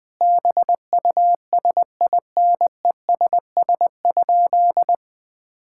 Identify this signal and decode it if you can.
Morse code, heard as a keyed tone, about 20 wpm, 700 Hz. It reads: BUSINESS?
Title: Sentences from Top 1000 Words - Review Part 5 - 20wpm